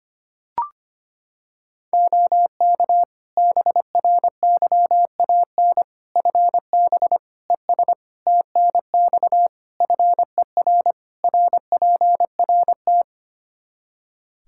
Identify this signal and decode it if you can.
Morse code, heard as a keyed tone, about 25 wpm, 700 Hz. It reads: OK BRYAN FB ES TNX FER RPRT
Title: Practice QSO - Exchange 2 - Info - 25wpm